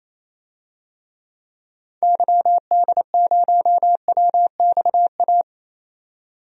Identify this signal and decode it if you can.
Morse code, heard as a keyed tone, about 27 wpm, 700 Hz. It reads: YD0WXA
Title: DXCC-100 Call Signs - 6 Characters - 28wpm